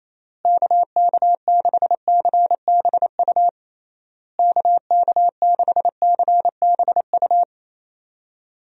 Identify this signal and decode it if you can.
Morse code, heard as a keyed tone, about 28 wpm, 700 Hz. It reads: KK6CBU KK6CBU
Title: USA Call Signs - 6 Characters - Repeated 3x - 28wpm